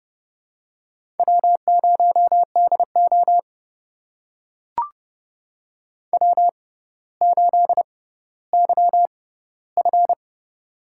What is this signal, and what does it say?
Morse code, heard as a keyed tone, about 30 wpm, spaced as for 15 wpm, 700 Hz. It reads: W0DO W8YF
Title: USA Call Signs - 4 Characters - 6x Character Spacing - 30wpm